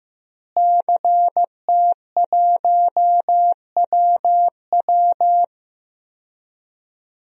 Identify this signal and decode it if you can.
Morse code, heard as a keyed tone, about 14 wpm, 700 Hz. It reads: CT1WW
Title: DXCC-100 Call Signs - 5 Characters - Repeated 3x - 15wpm